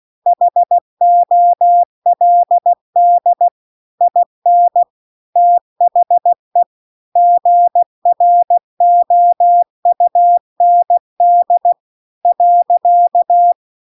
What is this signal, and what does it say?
Morse code, heard as a keyed tone, about 16 wpm, 700 Hz. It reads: HOLD IN THE GROUND .